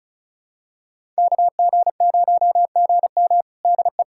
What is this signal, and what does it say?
Morse code, heard as a keyed tone, about 36 wpm, 700 Hz. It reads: KG0GM DE
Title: USA Callsign DE Callsign - 5 Characters - 35wpm